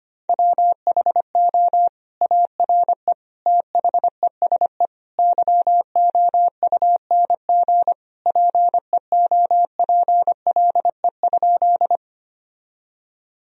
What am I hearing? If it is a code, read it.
Morse code, heard as a keyed tone, about 25 wpm, 700 Hz. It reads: WHO ARE THESE YOUNG PEOPLE?
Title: Sentences from Top 1000 Words - Review Part 1 - 25wpm